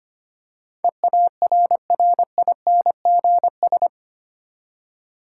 Morse code, heard as a keyed tone, about 25 wpm, 700 Hz. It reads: EARRINGS
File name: practice / 25wpm / Difficult Words with Hints - 25wpm